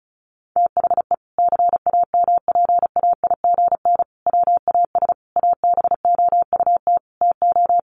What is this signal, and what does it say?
Morse code, heard as a keyed tone, about 34 wpm, 700 Hz. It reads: THE CAMPAIGN WAS ABOUT TO
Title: Sentences from Top 1200 Words 35wpm